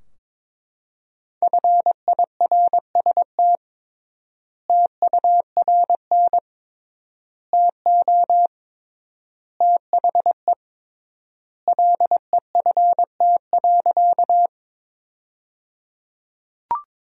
Morse code, heard as a keyed tone, about 22 wpm, 700 Hz. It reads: FIRST TURN TO THE LEFT.
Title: 3x Word-Spacing - Sentences from Top 300 Words - 22wpm